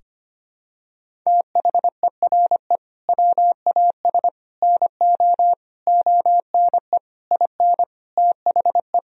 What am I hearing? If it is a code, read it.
Morse code, heard as a keyed tone, about 25 wpm, 700 Hz. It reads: THERE WAS NO ONE IN THE
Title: Sentences from Top 1000 Words - Review Part 4 - 25wpm